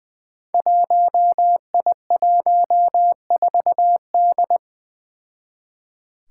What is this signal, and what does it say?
Morse code, heard as a keyed tone, about 20 wpm, 700 Hz. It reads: 1I14D
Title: Five Challenging Letter-Number Combinations 20wpm